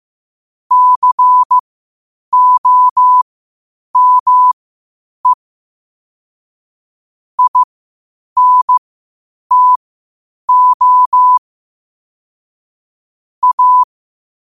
Morse code, heard as a keyed tone, about 15 wpm, spaced as for 8 wpm, 1000 Hz. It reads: COME INTO A
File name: qotdfiles / 8wpm